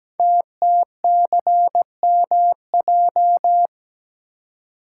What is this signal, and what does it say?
Morse code, heard as a keyed tone, about 17 wpm, 700 Hz. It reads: TTCMJ
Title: Code Groups - 5 Letters - 17wpm